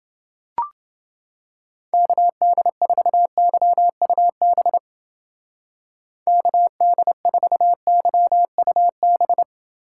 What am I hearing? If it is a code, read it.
Morse code, heard as a keyed tone, about 29 wpm, 700 Hz. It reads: KD4YUB KD4YUB